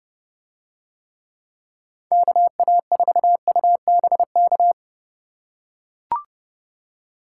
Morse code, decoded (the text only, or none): KA4UBK